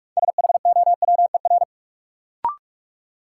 Morse code, heard as a keyed tone, about 43 wpm, 700 Hz. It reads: SHOWER
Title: Sentences from Top 1600 Words - New Word Drill - 45wpm